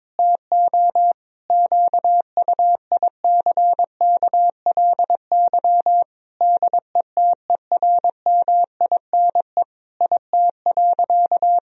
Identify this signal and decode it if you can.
Morse code, heard as a keyed tone, about 22 wpm, 700 Hz. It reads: TO QUICKLY DETERMINE IT.
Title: Sentences from Top 1000 Words - Review Part 4 - 22wpm